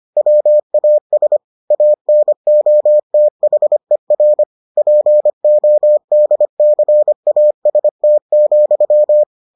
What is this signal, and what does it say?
Morse code, heard as a keyed tone, about 25 wpm, 600 Hz. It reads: WAS ANOTHER PODCAST,